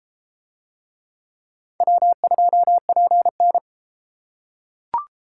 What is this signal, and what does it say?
Morse code, heard as a keyed tone, about 32 wpm, 700 Hz. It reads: W2PN